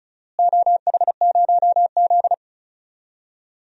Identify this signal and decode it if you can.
Morse code, heard as a keyed tone, about 36 wpm, 700 Hz. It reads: OH0Z
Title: DXCC-100 Call Signs - 4 Characters - Repeated 2x - 35wpm